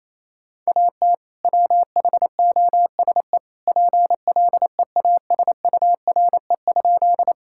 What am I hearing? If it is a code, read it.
Morse code, heard as a keyed tone, about 28 wpm, 700 Hz. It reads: AT WHOSE PLEASURE?
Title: Sentences from Top 1600 Words 28wpm